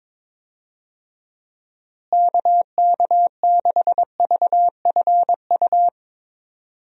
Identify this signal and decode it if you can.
Morse code, heard as a keyed tone, about 22 wpm, 700 Hz. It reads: KK6VFU
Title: USA Call Signs - 6 Characters - Repeated 3x - 22wpm